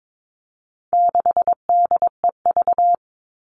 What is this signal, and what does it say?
Morse code, heard as a keyed tone, about 22 wpm, 700 Hz. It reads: 6DEV
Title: All Cut Numbers - 4-Digit 22wpm